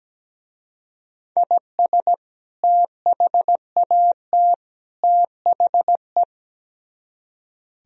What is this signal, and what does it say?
Morse code, heard as a keyed tone, about 17 wpm, 700 Hz. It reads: IS THAT THE
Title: Speed-Racing - Sets of 3 Words 17wpm